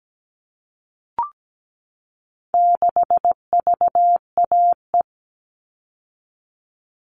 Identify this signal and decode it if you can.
Morse code, heard as a keyed tone, about 17 wpm, 700 Hz. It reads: 6VAE
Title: All Cut Numbers - 4-Digit 17wpm